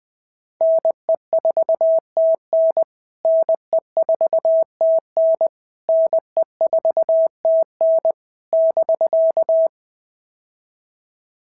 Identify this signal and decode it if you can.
Morse code, heard as a keyed tone, about 19 wpm, 650 Hz. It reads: NE4TN NE4TN NE4TN <BK>
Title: POTA QSO Practice - 20wpm